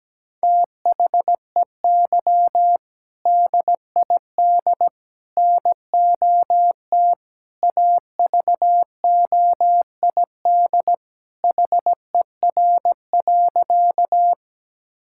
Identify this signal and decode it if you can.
Morse code, heard as a keyed tone, about 17 wpm, 700 Hz. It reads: THEY DID NOT AVOID HER.
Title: Sentences from Top 1000 Words - Review Part 6 - 17wpm